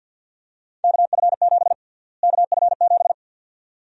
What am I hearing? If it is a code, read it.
Morse code, heard as a keyed tone, about 52 wpm, 700 Hz. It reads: KF7 KF7
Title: USA Call Sign Prefixes - Repeated 2x - 50wpm